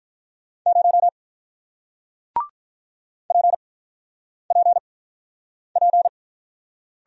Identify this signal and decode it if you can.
Morse code, heard as a keyed tone, about 46 wpm, spaced as for 9 wpm, 700 Hz. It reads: 0 PPP